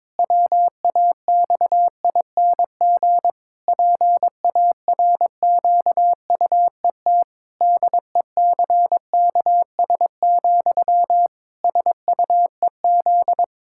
Morse code, heard as a keyed tone, about 22 wpm, 700 Hz. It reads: WAXING PARQUET DECKS, SUEZ
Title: Pangram Challenge v2 - 22wpm